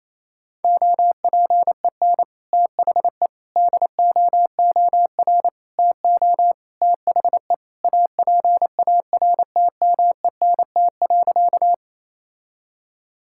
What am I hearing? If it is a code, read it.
Morse code, heard as a keyed tone, about 28 wpm, 700 Hz. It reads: OPEN THE DOOR TO THE APARTMENT.
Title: Sentences from Top 1000 Words - Review Part 6 - 28wpm